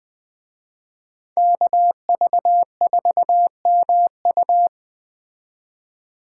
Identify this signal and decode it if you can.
Morse code, heard as a keyed tone, about 20 wpm, 700 Hz. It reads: KV4MU